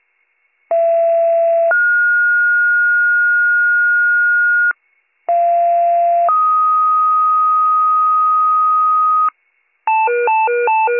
Two-tone page, heard as a one-second tone, then a three-second tone, about 670 then 1430 Hz, about 690 then 1250 Hz, with noise. It is heard from 0.7 to 9.3 s.